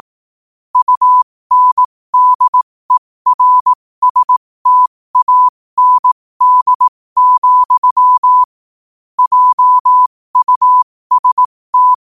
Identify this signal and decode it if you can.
Morse code, heard as a keyed tone, about 18 wpm, spaced as for 15 wpm, 1000 Hz. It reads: UNDERSTAND, JUST
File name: qotdfiles / 15wpm